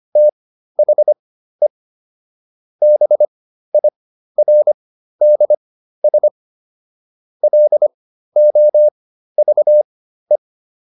Morse code, heard as a keyed tone, about 25 wpm, spaced as for 13 wpm, 600 Hz. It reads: THE BIRDS LOVE